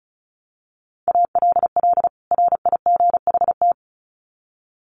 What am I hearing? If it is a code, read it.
Morse code, heard as a keyed tone, about 34 wpm, 700 Hz. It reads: ALL RIGHT
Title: Sets of 2 Words 35wpm